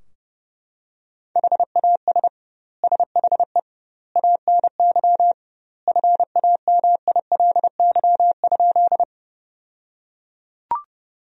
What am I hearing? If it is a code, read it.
Morse code, heard as a keyed tone, about 30 wpm, 700 Hz. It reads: HAS SHE ANY FAMILY?
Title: 2x Word-Spacing - Sentences from Top 400 Words - 30wpm